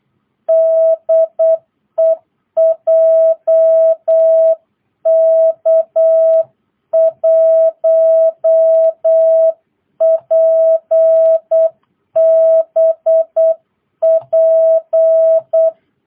Morse code, heard as a keyed tone, about 8 wpm, 660 Hz. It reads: DE JK1PBP